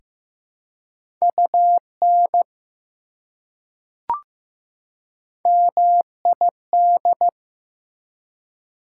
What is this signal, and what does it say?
Morse code, heard as a keyed tone, about 15 wpm, 700 Hz. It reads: UN MID
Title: Prefixes and Suffixes 15wpm